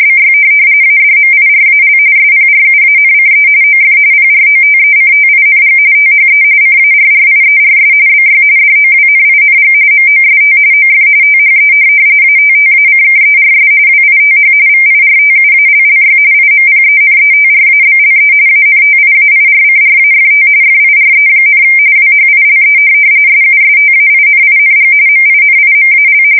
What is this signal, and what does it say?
RTTY, about 45.45 baud. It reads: BY ADMIRAL SPRUANCE OF PROTECTING THE CARRIERS BY DOWNING MOST OF THE ATTACKERS WITH VIRTUALLY IMPENETRABLE ANTIAIRCRAFT FIRE, WHILE SUSTAINING MINIMAL
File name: fsk170 / USS-New-Jersey